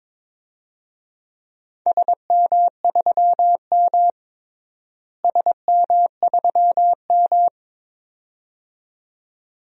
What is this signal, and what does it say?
Morse code, heard as a keyed tone, about 21 wpm, 700 Hz. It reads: SM3M SM3M
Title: DXCC-100 Call Signs - 4 Characters - Repeated 2x - 22wpm